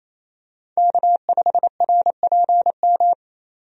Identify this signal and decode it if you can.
Morse code, heard as a keyed tone, about 27 wpm, 700 Hz. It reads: K5RPM
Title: USA Call Signs - 5 Characters - 28wpm